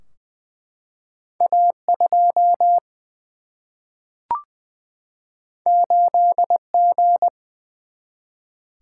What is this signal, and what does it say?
Morse code, heard as a keyed tone, about 21 wpm, 700 Hz. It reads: A 2 8 G
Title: Single-Letter Single-Number 20wpm